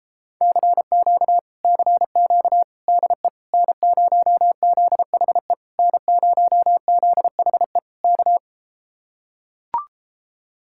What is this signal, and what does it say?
Morse code, heard as a keyed tone, about 33 wpm, 700 Hz. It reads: CQ CQ DE N0ZHE N0ZHE K